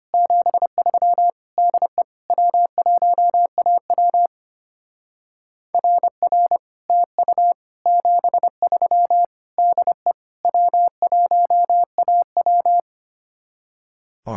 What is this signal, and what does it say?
Morse code, heard as a keyed tone, about 32 wpm, 700 Hz. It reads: 73 DE W1AW RR TU 73 DE W1AW